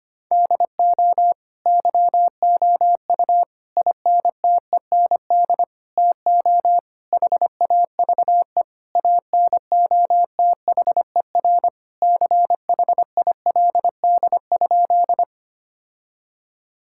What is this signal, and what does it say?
Morse code, heard as a keyed tone, about 25 wpm, 700 Hz. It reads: DO YOU INTEND TO HAVE ANOTHER CHILD?
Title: Sentences from Top 1600 Words 25wpm